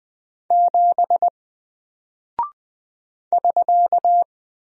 Morse code, heard as a keyed tone, about 20 wpm, 700 Hz. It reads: 7 <SK>
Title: Single Character (Letter:Number:Punctuation:Prosign) Alt 20wpm